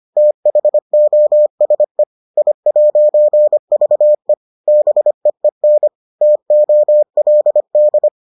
Morse code, heard as a keyed tone, about 25 wpm, 600 Hz. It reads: THOSE I'VE BEEN TOLD